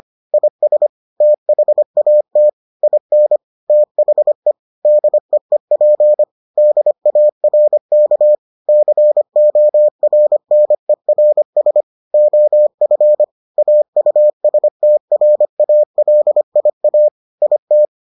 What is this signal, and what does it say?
Morse code, heard as a keyed tone, about 25 wpm, 600 Hz. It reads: IS THAT IN THE DEEP DARK CORNERS OF AUSTRALIA IT